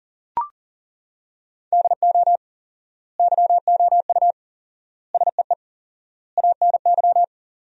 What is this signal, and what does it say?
Morse code, heard as a keyed tone, about 40 wpm, 700 Hz. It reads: DO YOU SEE ANY
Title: 4x Word-Spacing - Sentences from Top 500 Words - 40wpm